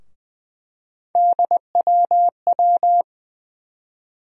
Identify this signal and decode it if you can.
Morse code, heard as a keyed tone, about 20 wpm, 700 Hz. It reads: DWW
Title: USA Call Sign Postfixes - 3 Letters 20wpm